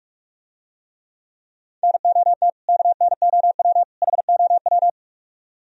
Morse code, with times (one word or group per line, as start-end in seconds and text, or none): 1.8-2.5 NOT
2.7-3.8 KNOW
4.0-4.9 HOW